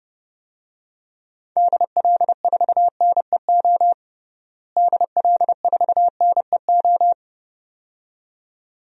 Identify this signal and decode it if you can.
Morse code, heard as a keyed tone, about 31 wpm, 700 Hz. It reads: DL4NEO DL4NEO
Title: DXCC-100 Call Signs - 6 Characters - Repeated 2x - 30wpm